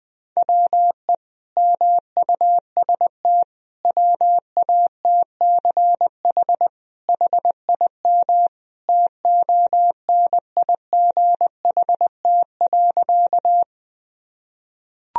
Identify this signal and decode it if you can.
Morse code, heard as a keyed tone, about 20 wpm, 700 Hz. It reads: WE MUST WATCH HIM TONIGHT.
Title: Sentences from Top 1000 Words - Review Part 3 - 20wpm